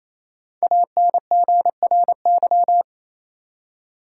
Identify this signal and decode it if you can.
Morse code, heard as a keyed tone, about 28 wpm, 700 Hz. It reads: ANGRY
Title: Five Letter Words - 28wpm